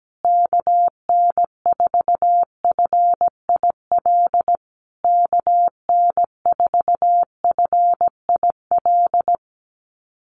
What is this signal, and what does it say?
Morse code, heard as a keyed tone, about 17 wpm, 700 Hz. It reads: KN4FIL KN4FIL
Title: USA Call Signs Repeated 3x 17wpm